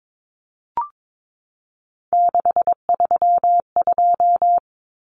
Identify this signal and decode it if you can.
Morse code, heard as a keyed tone, about 22 wpm, 700 Hz. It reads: 632
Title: Common Cut Numbers - 3-Digit 22wpm